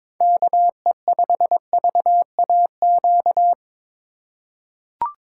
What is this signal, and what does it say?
Morse code, heard as a keyed tone, about 22 wpm, 700 Hz. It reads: KE5VAQ
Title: USA Call Signs - 6 Characters - 4x Character Spacing - 22wpm